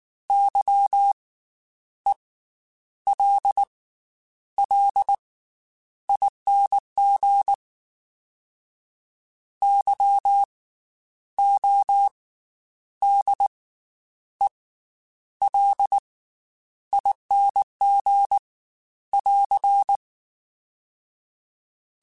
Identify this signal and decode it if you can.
Morse code, heard as a keyed tone, about 19 wpm, spaced as for 9 wpm, 800 Hz. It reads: YELLING YODELING <AR>
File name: Special INGs / Special INGs-mp3s / Slow ing @18